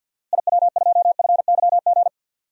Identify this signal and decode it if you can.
Morse code, heard as a keyed tone, about 48 wpm, 700 Hz. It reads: IW2FYG
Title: DXCC-100 Call Signs - 6 Characters - Repeated 3x - 50wpm